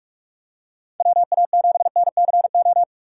Morse code, heard as a keyed tone, about 46 wpm, 700 Hz. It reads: WA7NCO